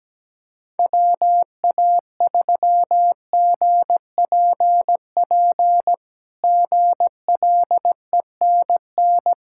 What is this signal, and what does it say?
Morse code, heard as a keyed tone, about 17 wpm, 700 Hz. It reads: WA3GPP GLENN
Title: K1USN SST QSO Practice - 17wpm